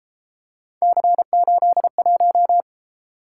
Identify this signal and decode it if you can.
Morse code, heard as a keyed tone, about 33 wpm, 700 Hz. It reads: C81